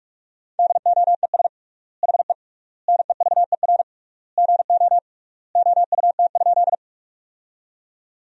Morse code, heard as a keyed tone, about 42 wpm, 700 Hz. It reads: DOES HE NEVER GO OUT?
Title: 3x Word-Spacing - Sentences from Top 300 Words - 45wpm